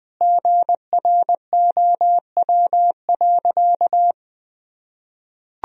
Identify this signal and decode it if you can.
Morse code, heard as a keyed tone, about 19 wpm, 700 Hz. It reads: GROW.
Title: Sentences from Top 1000 Words - Review Part 2 - 20wpm